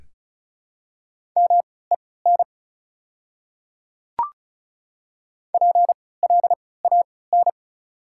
Morse code, heard as a keyed tone, about 34 wpm, spaced as for 20 wpm, 690 Hz. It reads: MEN PLAN
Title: 3x Character Spacing - Top 500 Words - 35wpm